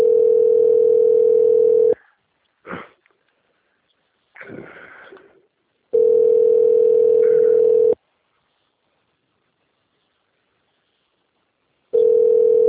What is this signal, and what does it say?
Ringback tone (North America), 440 + 480 Hz.